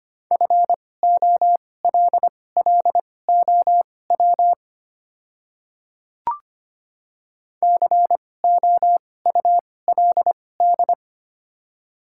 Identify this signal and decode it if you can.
Morse code, heard as a keyed tone, about 25 wpm, spaced as for 18 wpm, 700 Hz. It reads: FOLLOW COULD
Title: 2x Character Spacing - Top 200 Words - 25wpm